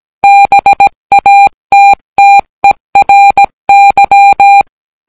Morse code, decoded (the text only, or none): BATTERY